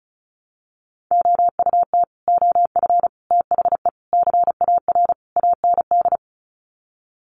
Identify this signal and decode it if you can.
Morse code, heard as a keyed tone, about 34 wpm, 700 Hz. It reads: OUT OF THE CAR AND
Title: Sets of 5 Words Encore 35wpm